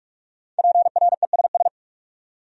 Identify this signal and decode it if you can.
Morse code, heard as a keyed tone, about 59 wpm, 700 Hz. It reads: PRESS